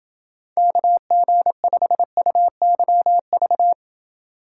Morse code, heard as a keyed tone, about 28 wpm, 700 Hz. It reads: KG5UYV